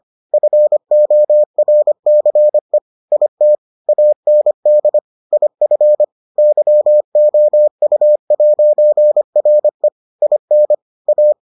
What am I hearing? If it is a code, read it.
Morse code, heard as a keyed tone, about 25 wpm, 600 Hz. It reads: FORCE IT AND IF YOU'RE IN A